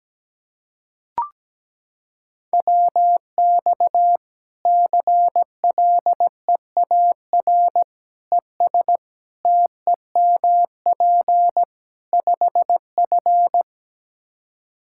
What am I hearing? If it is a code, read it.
Morse code, heard as a keyed tone, about 17 wpm, 700 Hz. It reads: WX CLEAR ES TEMP 5F
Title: Practice QSO - Exchange 2 - Info - 17wpm